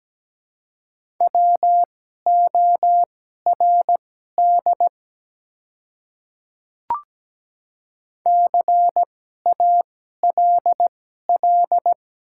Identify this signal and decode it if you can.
Morse code, heard as a keyed tone, about 17 wpm, spaced as for 12 wpm, 700 Hz. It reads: WORD CALL
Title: 2x Character Spacing - Top 100 Words - 17wpm